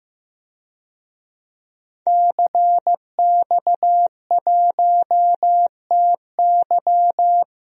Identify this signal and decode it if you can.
Morse code, heard as a keyed tone, about 15 wpm, 700 Hz. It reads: CX1TY
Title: DXCC-100 Call Signs - 5 Characters - 15wpm